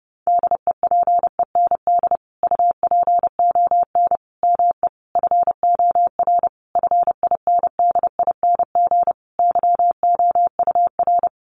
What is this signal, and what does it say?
Morse code, heard as a keyed tone, about 30 wpm, 700 Hz. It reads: DEPEND UPON ME FOR FINDING YOUR
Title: Sentences from Top 1300 Words 30wpm